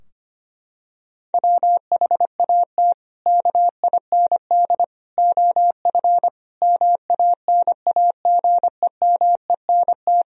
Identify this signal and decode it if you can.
Morse code, heard as a keyed tone, about 25 wpm, 700 Hz. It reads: WHAT KIND OF MANAGEMENT
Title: Sentences from Top 1000 Words - Review Part 4 - 25wpm